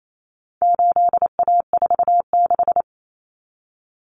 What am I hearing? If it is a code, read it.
Morse code, heard as a keyed tone, about 28 wpm, 700 Hz. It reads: 8A46
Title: Common Cut Numbers - 4-Digit 28wpm